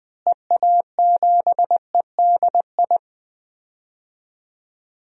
Morse code, heard as a keyed tone, about 20 wpm, 700 Hz. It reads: EA7EDI
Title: DXCC-100 Call Signs - 6 Characters - Repeated 2x - 20wpm